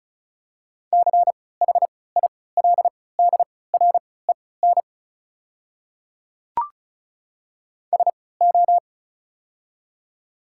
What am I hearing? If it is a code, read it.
Morse code, heard as a keyed tone, about 34 wpm, spaced as for 20 wpm, 700 Hz. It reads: CHILDREN SO
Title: 3x Character Spacing - Top 500 Words - 35wpm